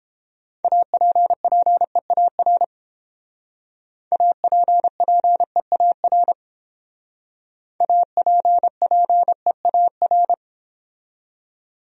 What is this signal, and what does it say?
Morse code, heard as a keyed tone, about 30 wpm, 700 Hz. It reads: APPEAR APPEAR APPEAR